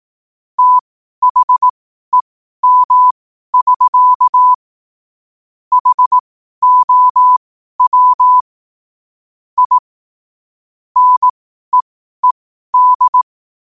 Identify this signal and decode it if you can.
Morse code, heard as a keyed tone, about 18 wpm, spaced as for 12 wpm, 1000 Hz. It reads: THEM<SK> HOW I NEED
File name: qotdfiles / 12wpm